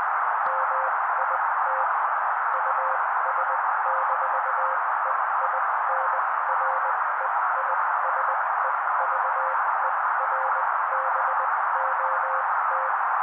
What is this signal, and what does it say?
Morse code, heard as a keyed tone, about 20 wpm, spaced as for 16 wpm, 530 Hz. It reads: MIT US-EINREISEVERBOT
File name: daily / 16wpm noise